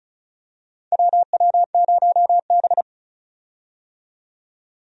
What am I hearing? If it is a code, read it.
Morse code, heard as a keyed tone, about 34 wpm, 700 Hz. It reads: WW0B
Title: USA Call Signs - 4 Characters - 35wpm